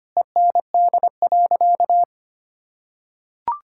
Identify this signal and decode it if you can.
Morse code, heard as a keyed tone, about 25 wpm, 700 Hz. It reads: END.
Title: Sentences from Top 1000 Words - Review Part 5 - 25wpm